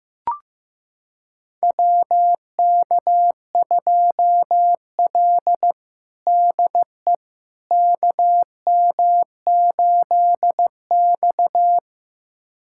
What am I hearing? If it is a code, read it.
Morse code, heard as a keyed tone, about 15 wpm, 700 Hz. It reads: WK2L DE KM8X
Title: USA Callsign DE Callsign - 4 Characters - 15wpm